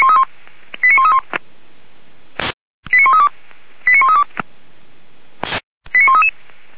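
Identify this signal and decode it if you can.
Five-tone selective call (ZVEI), heard as quick runs of five tones, tones of about 70 ms. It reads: 89131 89132 89132 89133
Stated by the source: ZVEI five-tone selective call received by SDR.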